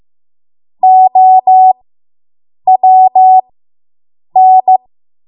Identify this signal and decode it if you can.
Morse code, heard as a keyed tone, about 15 wpm, spaced as for 7 wpm, 750 Hz. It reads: OWN